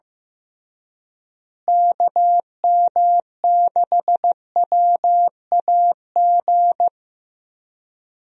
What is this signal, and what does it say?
Morse code, heard as a keyed tone, about 15 wpm, 700 Hz. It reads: KM6WAG
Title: USA Call Signs - 6 Characters - Repeated 3x - 15wpm